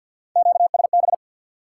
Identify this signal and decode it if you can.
Morse code, heard as a keyed tone, about 53 wpm, 700 Hz. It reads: QSB
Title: Top 90 QSO Elements 50wpm